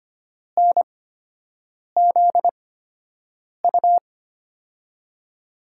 Morse code, heard as a keyed tone, about 25 wpm, spaced as for 7 wpm, 700 Hz. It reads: NZU
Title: USA Call Signs - 6 Characters - 8x Character Spacing - 25wpm